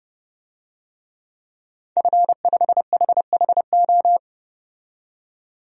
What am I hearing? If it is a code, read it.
Morse code, heard as a keyed tone, about 30 wpm, 700 Hz. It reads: F5HHO
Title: DXCC-100 Call Signs - 5 Characters - 30wpm